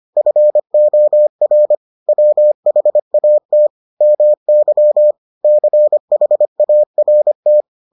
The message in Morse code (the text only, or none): FOR WHAT MY CHART